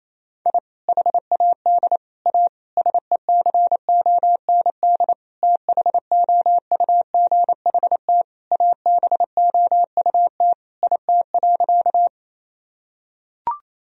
Morse code, heard as a keyed tone, about 28 wpm, 700 Hz. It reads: I HAD A SECOND THOUGHT ABOUT IT.
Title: Sentences from Top 1000 Words - Review Part 1 - 28wpm